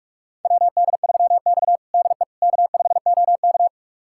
Morse code, heard as a keyed tone, about 46 wpm, 700 Hz. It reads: WD3X DE K5OK
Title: USA Callsign DE Callsign - 4 Characters - 45wpm